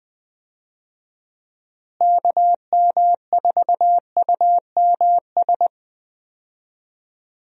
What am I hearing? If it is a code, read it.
Morse code, heard as a keyed tone, about 20 wpm, 700 Hz. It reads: KM4UMS
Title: USA Call Signs - 6 Characters - Repeated 2x - 20wpm